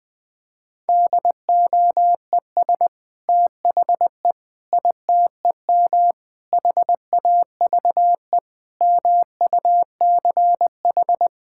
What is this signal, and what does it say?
Morse code, heard as a keyed tone, about 20 wpm, 700 Hz. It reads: DOES THE ITEM HAVE MUCH
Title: Sentences from Top 1000 Words - Review Part 6 - 20wpm